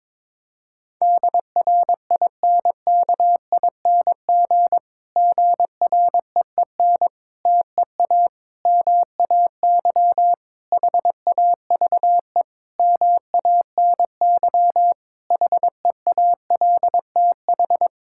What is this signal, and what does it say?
Morse code, heard as a keyed tone, about 22 wpm, 700 Hz. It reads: DRINKING GREEN TEA MAY HAVE MANY HEALTH
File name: Difficult Sentences with Hints - 22wpm